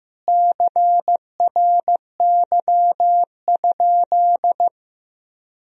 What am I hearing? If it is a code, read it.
Morse code, heard as a keyed tone, about 15 wpm, 700 Hz. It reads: CRY?
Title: Sentences from Top 1000 Words - Review Part 2 - 15wpm